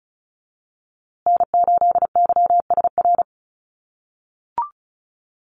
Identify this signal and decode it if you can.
Morse code, heard as a keyed tone, about 34 wpm, 700 Hz. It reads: N8YSR
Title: USA Call Signs Repeated 3x 35wpm